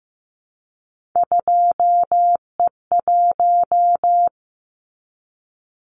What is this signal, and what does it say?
Morse code, heard as a keyed tone, about 15 wpm, 700 Hz. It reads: 2E1
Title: Callsign Prefixes for Top 25 DXCC - Repeated 3x - 15wpm